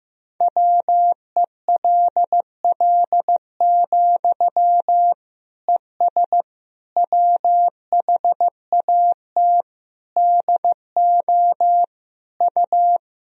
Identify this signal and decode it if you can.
Morse code, heard as a keyed tone, about 15 wpm, 700 Hz. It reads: WELL, ES WHAT DO U
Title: Sentences from Top 200 Words and Abbreviations 15wpm